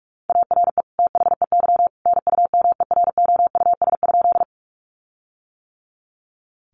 Morse code, heard as a keyed tone, about 44 wpm, 700 Hz. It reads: ARE THEY NUMEROUS?
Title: Sentences from Top 1400 Words 45wpm